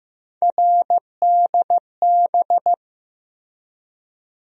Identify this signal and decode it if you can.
Morse code, heard as a keyed tone, about 15 wpm, 700 Hz. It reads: RDB
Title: USA Call Sign Postfixes - 3 Letters 15wpm